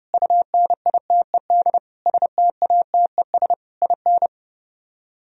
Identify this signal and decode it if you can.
Morse code, heard as a keyed tone, about 36 wpm, 700 Hz. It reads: UNITED STATES IN
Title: Speed-Racing - Sets of 5 Words 30wpm